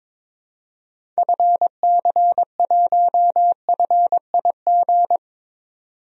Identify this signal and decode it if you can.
Morse code, heard as a keyed tone, about 22 wpm, 700 Hz. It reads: FC1FIG
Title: DXCC-100 Call Signs - 6 Characters - 22wpm